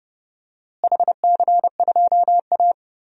Morse code, heard as a keyed tone, about 30 wpm, 700 Hz. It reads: HC2A
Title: DXCC-100 Call Signs - 4 Characters - Repeated 3x - 30wpm